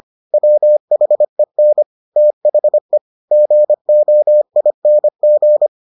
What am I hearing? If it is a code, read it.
Morse code, heard as a keyed tone, about 25 wpm, 600 Hz. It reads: WHEN THE GOING